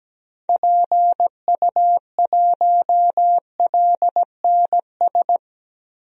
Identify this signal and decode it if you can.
Morse code, heard as a keyed tone, about 17 wpm, 700 Hz. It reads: PU1LNS
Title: DXCC-100 Call Signs - 6 Characters - Repeated 3x - 17wpm